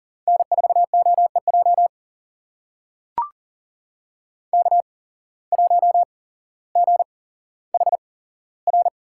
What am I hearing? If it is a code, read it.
Morse code, heard as a keyed tone, about 37 wpm, spaced as for 17 wpm, 700 Hz. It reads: N4OEJ K1GHR
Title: USA Call Signs - 5 Characters - 8x Character Spacing - 40wpm